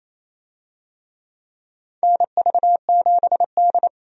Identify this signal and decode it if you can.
Morse code, heard as a keyed tone, about 28 wpm, 700 Hz. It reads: NV7D
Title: USA Call Signs - 4 Characters - 28wpm